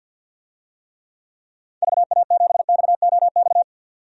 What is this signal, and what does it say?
Morse code, heard as a keyed tone, about 52 wpm, 700 Hz. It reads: VA7XOX
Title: DXCC-100 Call Signs - 6 Characters - 50wpm